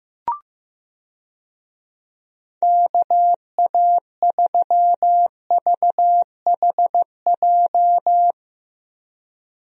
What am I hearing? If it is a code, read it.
Morse code, heard as a keyed tone, about 15 wpm, 700 Hz. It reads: KA3VHJ
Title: USA Call Signs - 6 Characters - Repeated 2x - 15wpm